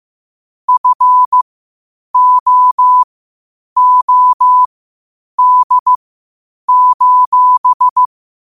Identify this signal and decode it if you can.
Morse code, heard as a keyed tone, about 15 wpm, spaced as for 8 wpm, 1000 Hz. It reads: FOOD:
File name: qotdfiles / 8wpm